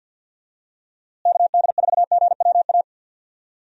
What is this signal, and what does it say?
Morse code, heard as a keyed tone, about 52 wpm, 700 Hz. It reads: KD4GWA
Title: USA Call Signs - 6 Characters - Repeated 2x - 50wpm